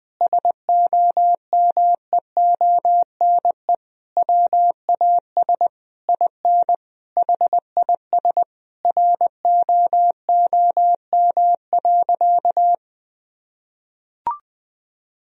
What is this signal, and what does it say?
Morse code, heard as a keyed tone, about 20 wpm, 700 Hz. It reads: SOMEONE WAS IN HIS ROOM.
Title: Sentences from Top 1000 Words - Review Part 4 - 20wpm